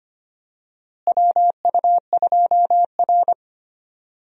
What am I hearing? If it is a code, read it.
Morse code, heard as a keyed tone, about 25 wpm, 700 Hz. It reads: WU2R
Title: USA Call Signs - 4 Characters - Repeated 2x - 25wpm